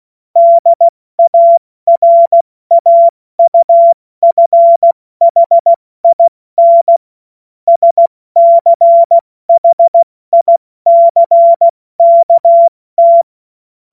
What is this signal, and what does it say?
Morse code, heard as a keyed tone, about 16 wpm, 680 Hz. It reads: DARAUFHIN SCHICKT
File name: Morsekurs / Witze / Witz 14. September 2021, 70 F80